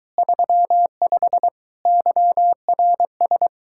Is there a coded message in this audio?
Yes: Morse code from 0.2 to 3.5 s, heard as a keyed tone, about 23 wpm, 700 Hz. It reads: 35 YRS